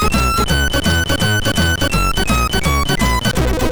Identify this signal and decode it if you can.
There is no code.